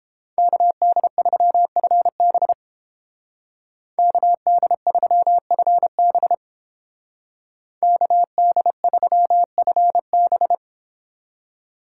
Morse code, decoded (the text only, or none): KD3FB KD3FB KD3FB